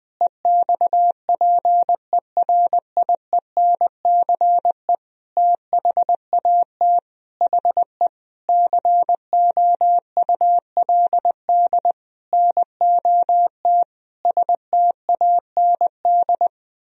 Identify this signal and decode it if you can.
Morse code, heard as a keyed tone, about 20 wpm, 700 Hz. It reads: EXPERIENCE THAT HE COULD NOT STAND
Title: Sentences from Top 1000 Words - Review Part 3 - 20wpm